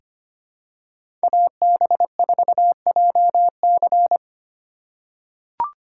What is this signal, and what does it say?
Morse code, heard as a keyed tone, about 25 wpm, 700 Hz. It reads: AB4JC
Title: USA Call Signs - 5 Characters - 3x Character Spacing - 25wpm